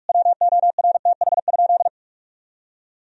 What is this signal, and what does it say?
Morse code, heard as a keyed tone, about 46 wpm, 700 Hz. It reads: WORTH?
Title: Sentences from Top 1000 Words - Review Part 4 - 45wpm